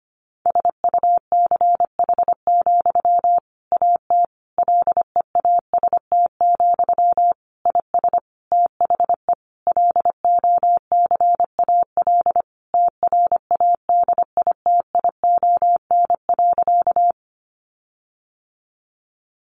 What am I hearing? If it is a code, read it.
Morse code, heard as a keyed tone, about 25 wpm, 700 Hz. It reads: SUCH, AT LEAST, IS THE LOCAL TRADITION.
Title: Sentences from Top 1500 Words 25wpm